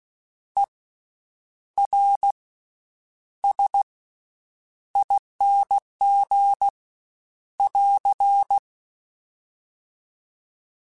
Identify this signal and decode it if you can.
Morse code, heard as a keyed tone, about 16 wpm, spaced as for 8 wpm, 800 Hz. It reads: ERSING <AR>
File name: Special INGs-mp3s / Special INGs-mp3s / Slow ing @15